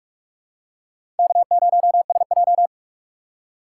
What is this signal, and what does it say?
Morse code, heard as a keyed tone, about 46 wpm, 700 Hz. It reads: K0SJ
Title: USA Call Signs - 4 Characters - 3x Character Spacing - 45wpm